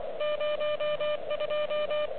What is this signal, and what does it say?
Morse code, heard as a keyed tone, about 24 wpm, 600 Hz, with noise. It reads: 02
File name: OK QRP 2022 / Zvuk / REC31